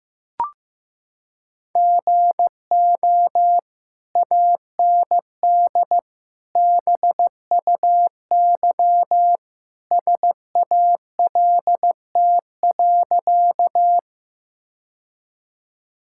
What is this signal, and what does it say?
Morse code, heard as a keyed tone, about 16 wpm, 700 Hz. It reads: GO AND BUY SALT.
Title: Sentences from Top 1000 Words - Review Part 5 - 15wpm